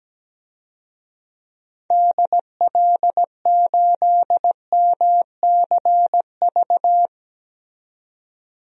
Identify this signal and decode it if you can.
Morse code, heard as a keyed tone, about 17 wpm, 700 Hz. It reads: DL8MCV
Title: DXCC-100 Call Signs - 6 Characters - 17wpm